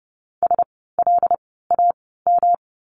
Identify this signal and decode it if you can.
Morse code, heard as a keyed tone, about 30 wpm, spaced as for 17 wpm, 700 Hz. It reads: SLAM
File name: course / 30:10wpm / 108 Lesson 18f - Letter m - Words 30wpm